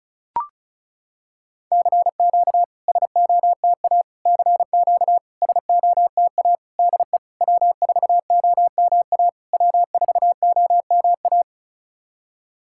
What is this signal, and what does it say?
Morse code, heard as a keyed tone, about 34 wpm, 690 Hz. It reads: CQ SOTA CQ SOTA DE W4OMA W4OMA
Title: SOTA QSO Practice 35wpm